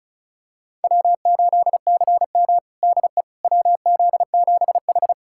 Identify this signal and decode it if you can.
Morse code, heard as a keyed tone, about 36 wpm, 700 Hz. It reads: W8CM DE WZ7H
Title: USA Callsign DE Callsign - 4 Characters - 35wpm